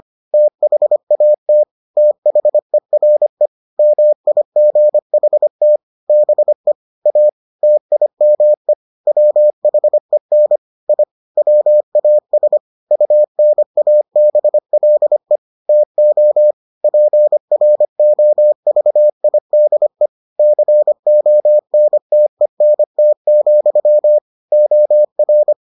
Morse code, heard as a keyed tone, about 25 wpm, 600 Hz. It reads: THAT THERE MIGHT BE A TIME WHEN I WAS UNABLE TO PROVIDE CONTENT, OR